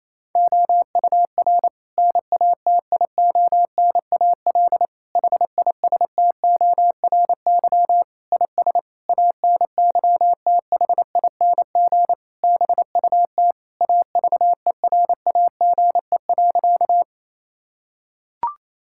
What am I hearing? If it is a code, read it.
Morse code, heard as a keyed tone, about 28 wpm, 700 Hz. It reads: OUR NATIONAL HISTORY IS ANYTHING BUT AVERAGE.
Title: Sentences from Top 1000 Words - Review Part 6 - 28wpm